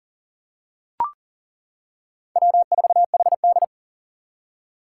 Morse code, heard as a keyed tone, about 40 wpm, 700 Hz. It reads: W4HD
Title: Four Challenging Letter-Number Combinations 40wpm